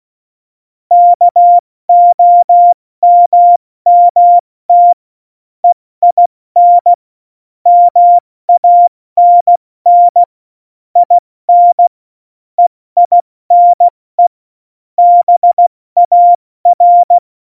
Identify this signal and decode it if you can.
Morse code, heard as a keyed tone, about 16 wpm, spaced as for 14 wpm, 700 Hz. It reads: KOMMT EIN MANN IN EINE BAR
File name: Morsekurs / Witze / Witz 15. Oktober 2019, 70 F80